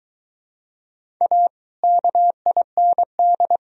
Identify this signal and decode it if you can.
Morse code, heard as a keyed tone, about 18 wpm, 700 Hz. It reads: A KIND
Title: Speed-Racing - Sets of 3 Words 15wpm